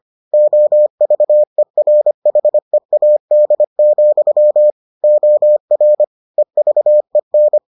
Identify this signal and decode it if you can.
Morse code, heard as a keyed tone, about 25 wpm, 600 Hz. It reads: OVERHEAD, OR EVEN